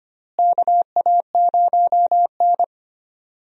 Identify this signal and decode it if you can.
Morse code, heard as a keyed tone, about 25 wpm, 700 Hz. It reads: KA0N
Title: USA Call Signs - 4 Characters - Repeated 3x - 25wpm